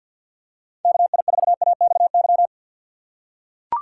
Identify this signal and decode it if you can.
Morse code, heard as a keyed tone, about 52 wpm, 700 Hz. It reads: KI4AXY
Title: USA Call Signs - 6 Characters - 6x Character Spacing - 50wpm